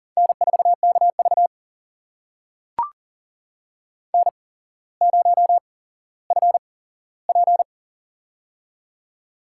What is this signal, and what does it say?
Morse code, heard as a keyed tone, about 38 wpm, spaced as for 17 wpm, 700 Hz. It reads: N4KV N0FP
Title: USA Call Signs - 4 Characters - 8x Character Spacing - 40wpm